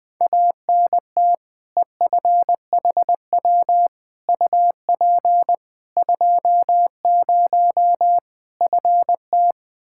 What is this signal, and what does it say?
Morse code, heard as a keyed tone, about 20 wpm, 700 Hz. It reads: ANT EFHW UP 20 FT
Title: Practice QSO - Exchange 2 - Info - 20wpm